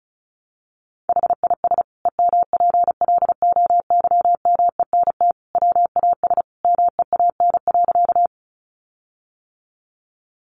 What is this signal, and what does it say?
Morse code, heard as a keyed tone, about 34 wpm, 700 Hz. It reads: HIS EMPLOYMENT WAS MEAN.
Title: Sentences from Top 1500 Words 35wpm